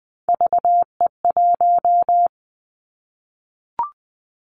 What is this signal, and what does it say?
Morse code, heard as a keyed tone, about 19 wpm, 700 Hz. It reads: VE1
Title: Callsign Prefixes for Top 100 DXCC - 20wpm